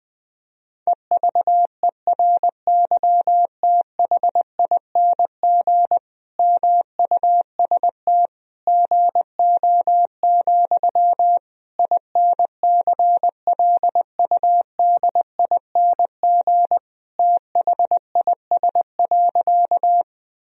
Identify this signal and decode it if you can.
Morse code, heard as a keyed tone, about 20 wpm, 700 Hz. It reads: EVERYTHING MUST GO, INCLUDING THIS.
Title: Sentences from Top 1000 Words - Review Part 4 - 20wpm